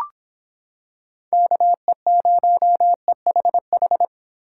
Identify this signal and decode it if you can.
Morse code, heard as a keyed tone, about 26 wpm, 700 Hz. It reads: KE0EHH